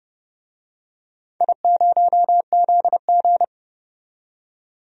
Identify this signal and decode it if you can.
Morse code, heard as a keyed tone, about 30 wpm, 700 Hz. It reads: I0ZG
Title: DXCC-100 Call Signs - 4 Characters - 30wpm